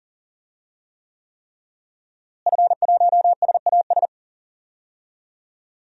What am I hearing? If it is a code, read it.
Morse code, heard as a keyed tone, about 37 wpm, 700 Hz. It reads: F1SAS
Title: DXCC-100 Call Signs - 5 Characters - Repeated 3x - 40wpm